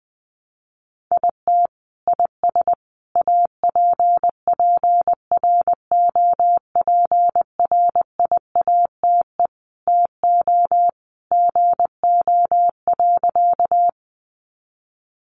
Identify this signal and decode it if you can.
Morse code, heard as a keyed tone, about 20 wpm, 700 Hz. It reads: IT IS APPROPRIATE TO GO.
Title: Sentences from Top 900 Words 20wpm